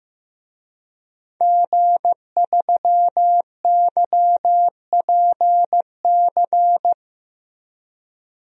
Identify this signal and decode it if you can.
Morse code, heard as a keyed tone, about 15 wpm, 700 Hz. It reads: G3YPC
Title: DXCC-100 Call Signs - 5 Characters - Repeated 2x - 15wpm